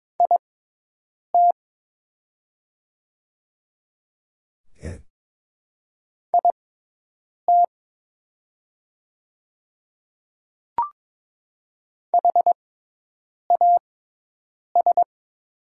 Morse code, heard as a keyed tone, about 22 wpm, spaced as for 8 wpm, 700 Hz. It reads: IT IT HAS
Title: 6x Character Spacing - Top 100 Words - 22wpm